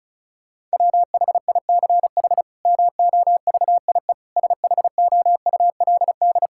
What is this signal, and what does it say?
Morse code, heard as a keyed tone, about 34 wpm, 700 Hz. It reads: WHICH MOVIE SHOULD
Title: Sentences from Top 1000 Words - Review Part 5 - 35wpm